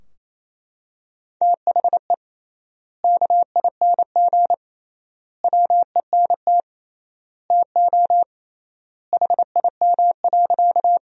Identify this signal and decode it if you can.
Morse code, heard as a keyed tone, about 28 wpm, 700 Hz. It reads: THE KING WENT TO HIM.
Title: 3x Word-Spacing - Sentences from Top 400 Words - 28wpm